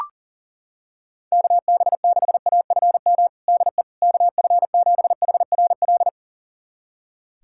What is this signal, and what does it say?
Morse code, heard as a keyed tone, about 40 wpm, 700 Hz. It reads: KB6AFM DE KF7HRL
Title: USA Callsign DE Callsign - 6 Characters - 40wpm